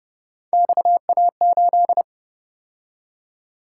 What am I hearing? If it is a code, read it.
Morse code, heard as a keyed tone, about 30 wpm, 700 Hz. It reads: XA8